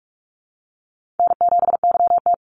Morse code, heard as a keyed tone, about 44 wpm, 700 Hz. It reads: N7YT
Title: Easy Call Signs Repeated 3x - 45wpm